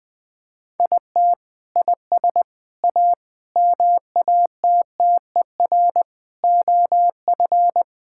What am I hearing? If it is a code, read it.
Morse code, heard as a keyed tone, about 20 wpm, 700 Hz. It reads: IT IS A MATTER OF
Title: Sentences from Top 1000 Words - Review Part 5 - 20wpm